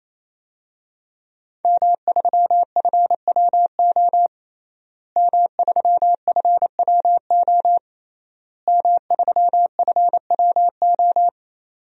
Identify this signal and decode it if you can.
Morse code, heard as a keyed tone, about 28 wpm, 700 Hz. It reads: M3FWO M3FWO M3FWO
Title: DXCC-100 Call Signs - 5 Characters - Repeated 3x - 28wpm